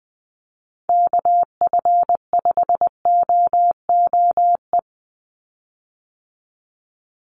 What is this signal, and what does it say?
Morse code, heard as a keyed tone, about 20 wpm, 700 Hz. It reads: KF5OOE
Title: USA Call Signs 20wpm